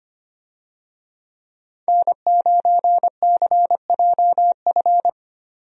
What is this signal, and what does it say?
Morse code, heard as a keyed tone, about 25 wpm, 700 Hz. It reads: N9CJF
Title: USA Call Signs - 5 Characters - 25wpm